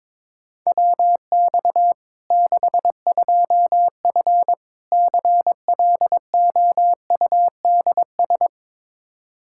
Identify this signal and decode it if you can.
Morse code, heard as a keyed tone, about 21 wpm, 700 Hz. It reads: WX 62F CLOUDS
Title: K1USN SST QSO Practice 22wpm